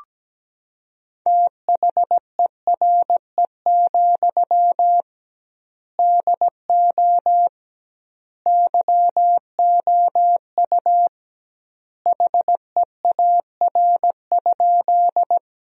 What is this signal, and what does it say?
Morse code, heard as a keyed tone, about 17 wpm, 700 Hz. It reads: THERE, DO YOU HEAR?
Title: 2x Word-Spacing - Sentences from Top 300 Words - 17wpm